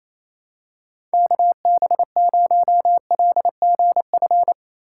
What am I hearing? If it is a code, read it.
Morse code, heard as a keyed tone, about 28 wpm, 700 Hz. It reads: KB0LGF
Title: K1USN SST QSO Practice - 28wpm